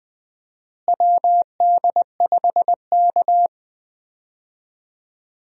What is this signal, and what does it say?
Morse code, heard as a keyed tone, about 20 wpm, 700 Hz. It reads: WD5K
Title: USA Call Signs - 4 Characters - Repeated 2x - 20wpm